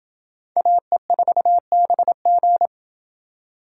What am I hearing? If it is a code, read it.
Morse code, heard as a keyed tone, about 29 wpm, 700 Hz. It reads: AE4BG